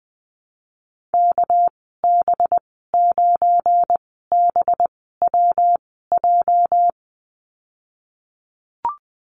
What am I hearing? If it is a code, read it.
Morse code, heard as a keyed tone, about 19 wpm, spaced as for 14 wpm, 700 Hz. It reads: KB9BWJ
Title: 231 Lesson 37g - Letter x - Call Signs 20wpm